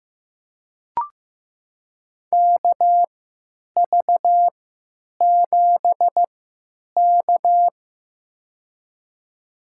Morse code, heard as a keyed tone, about 15 wpm, spaced as for 8 wpm, 700 Hz. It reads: KV7K
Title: USA Call Signs - 4 Characters - 3x Character Spacing - 15wpm